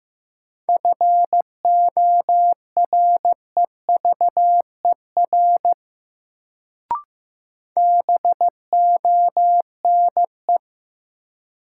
Morse code, heard as a keyed tone, about 15 wpm, 700 Hz. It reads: FOREVER BONE
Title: Sentences from Top 1600 Words - New Word Drill - 15wpm